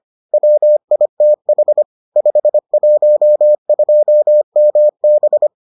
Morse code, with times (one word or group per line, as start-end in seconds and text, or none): 0.3-1.8 WITH
2.2-5.5 512MB